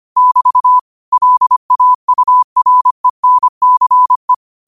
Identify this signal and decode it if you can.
Morse code, heard as a keyed tone, about 25 wpm, 1000 Hz. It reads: <BT> LAURENCE